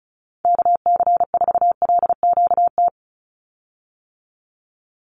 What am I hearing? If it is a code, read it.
Morse code, heard as a keyed tone, about 34 wpm, 700 Hz. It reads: KC4LQT
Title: USA Call Signs 35wpm